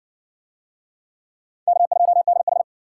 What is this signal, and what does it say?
Morse code, heard as a keyed tone, about 66 wpm, 700 Hz. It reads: K2DH